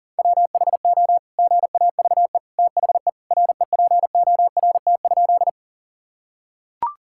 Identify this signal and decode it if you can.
Morse code, heard as a keyed tone, about 40 wpm, 700 Hz. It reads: WHO GAVE THE REPORT?
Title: Sentences from Top 1000 Words - Review Part 6 - 40wpm